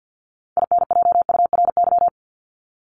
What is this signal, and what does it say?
Morse code, heard as a keyed tone, about 50 wpm, 700 Hz. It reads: INJURY
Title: Top 1751-2000 Words 50wpm